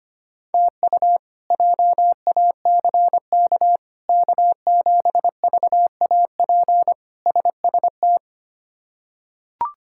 Morse code, heard as a keyed tone, about 25 wpm, 700 Hz. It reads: TU JACK K7VAP SST
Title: K1USN SST QSO Practice - 25wpm